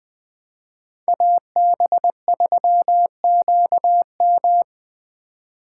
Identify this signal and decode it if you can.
Morse code, heard as a keyed tone, about 20 wpm, 700 Hz. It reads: AB3QM
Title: USA Call Signs - 5 Characters - 2x Character Spacing - 20wpm